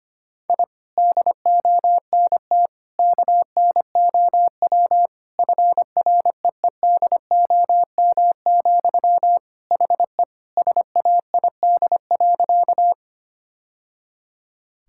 Morse code, heard as a keyed tone, about 25 wpm, 700 Hz. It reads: I DONT KNOW FREEDOM, HE SAID.
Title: Sentences from Top 1000 Words - Review Part 5 - 25wpm